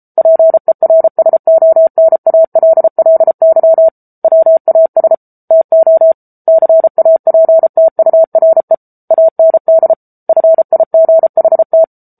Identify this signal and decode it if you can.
Morse code, heard as a keyed tone, about 34 wpm, 650 Hz. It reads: PERSONALLY WAS TO CAPTURE AND FIGHT
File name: Book Two / 07-Chapter 7 - 33 WPM